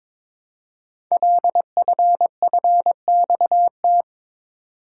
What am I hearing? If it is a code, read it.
Morse code, heard as a keyed tone, about 22 wpm, 700 Hz. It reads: LFFXT
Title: Code Groups - 5 Letters - 22wpm